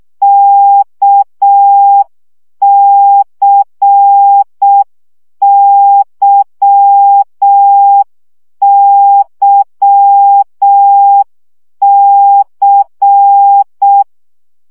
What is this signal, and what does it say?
Morse code, heard as a keyed tone, about 6 wpm, 800 Hz. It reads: KCYYC
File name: CW gyakorlas k c y 5 baud